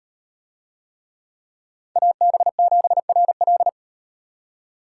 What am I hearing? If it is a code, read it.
Morse code, heard as a keyed tone, about 38 wpm, 700 Hz. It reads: AB7RL